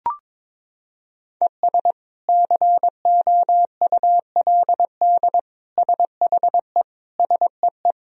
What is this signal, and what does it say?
Morse code, heard as a keyed tone, about 22 wpm, 700 Hz. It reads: ES COULD SHE SEE
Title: Sentences from Top 100 Words and Abbreviations 22wpm